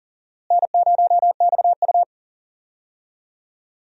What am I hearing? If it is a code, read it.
Morse code, heard as a keyed tone, about 40 wpm, 700 Hz. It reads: N0XU
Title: USA Call Signs - 4 Characters - Repeated 2x - 40wpm